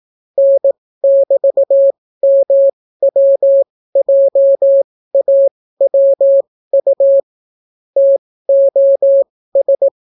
Morse code, heard as a keyed tone, about 18 wpm, spaced as for 14 wpm, 550 Hz. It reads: N<BT>MWJAWU TOS